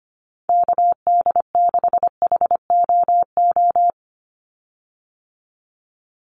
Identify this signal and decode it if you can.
Morse code, heard as a keyed tone, about 25 wpm, 700 Hz. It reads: KD6HOO
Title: USA Call Signs 25wpm